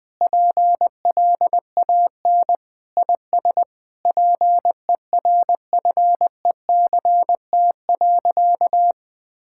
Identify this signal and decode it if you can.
Morse code, heard as a keyed tone, about 20 wpm, 700 Hz. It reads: PLAN IS PERFECT.
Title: Sentences with 7-Word Phrases - 20wpm